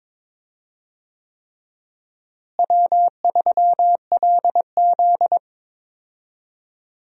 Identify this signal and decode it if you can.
Morse code, heard as a keyed tone, about 23 wpm, 700 Hz. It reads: W3LZ
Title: USA Call Signs - 4 Characters - Repeated 3x - 22wpm